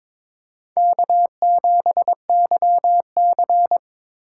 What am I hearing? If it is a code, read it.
Morse code, heard as a keyed tone, about 22 wpm, 700 Hz. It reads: K7YC
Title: USA Call Signs - 3x Character Spacing - 22wpm